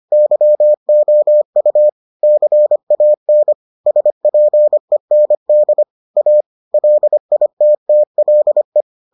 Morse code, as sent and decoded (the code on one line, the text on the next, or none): -.-- --- ..- / -.-. .- -. / ... .--. -. -.. / .- / .-.. .. - - .-.. .
YOU CAN SPEND A LITTLE